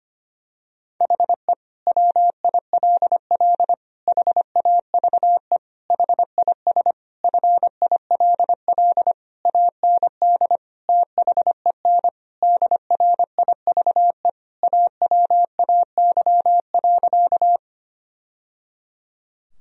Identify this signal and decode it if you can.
Morse code, heard as a keyed tone, about 25 wpm, 700 Hz. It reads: HE WILL HAVE HIS FILL AND THEN DRIVE AWAY.
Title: Sentences from Top 1000 Words - Review Part 3 - 25wpm